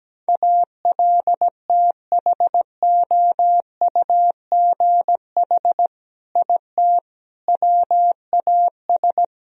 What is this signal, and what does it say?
Morse code, heard as a keyed tone, about 17 wpm, 700 Hz. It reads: ALTHOUGH IT WAS
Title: Sentences from Top 1000 Words - Review Part 3 - 17wpm